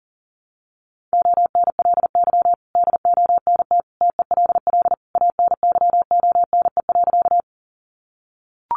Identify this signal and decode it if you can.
Morse code, heard as a keyed tone, about 40 wpm, 700 Hz. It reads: ONLY DONT TELL ANYONE.
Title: Sentences from Top 900 Words 40wpm